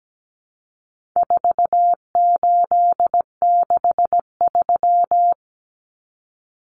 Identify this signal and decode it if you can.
Morse code, heard as a keyed tone, about 17 wpm, 700 Hz. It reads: 4863
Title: Common Cut Numbers - 4-Digit 17wpm